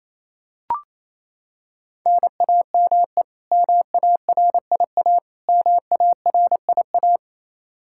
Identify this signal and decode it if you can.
Morse code, heard as a keyed tone, about 28 wpm, 700 Hz. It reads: NAME MARIA MARIA
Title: Practice QSO - Exchange 1 - Beginning - 28wpm